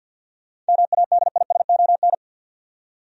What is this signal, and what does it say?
Morse code, heard as a keyed tone, about 47 wpm, 700 Hz. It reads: MADISON
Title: USA Common Names - 50wpm